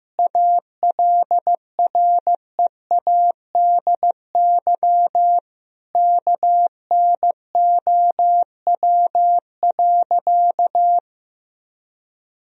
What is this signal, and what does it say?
Morse code, heard as a keyed tone, about 15 wpm, 700 Hz. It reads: ALREADY KNOW.
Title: Sentences from Top 1000 Words - Review Part 1 - 15wpm